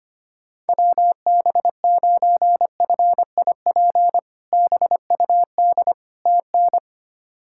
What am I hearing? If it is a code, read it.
Morse code, heard as a keyed tone, about 25 wpm, 700 Hz. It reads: WB9FIP BUD TN
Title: K1USN SST QSO Practice - 25wpm